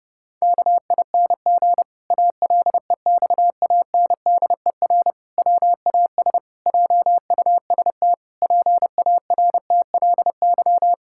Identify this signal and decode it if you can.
Morse code, heard as a keyed tone, about 30 wpm, 700 Hz. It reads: KING ALEXANDER WAS JUST PARTLY
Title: Pangram Challenge v2 - 30wpm